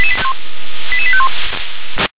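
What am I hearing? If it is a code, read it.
Five-tone selective call (ZVEI), heard as quick runs of five tones, tones of about 70 ms. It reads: 88851
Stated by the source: ZVEI five-tone selective call received on an SDR.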